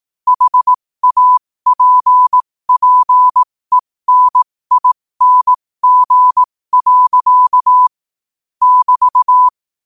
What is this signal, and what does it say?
Morse code, heard as a keyed tone, about 18 wpm, spaced as for 15 wpm, 1000 Hz. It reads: HAPPENING. <BT>
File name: qotdfiles / 15wpm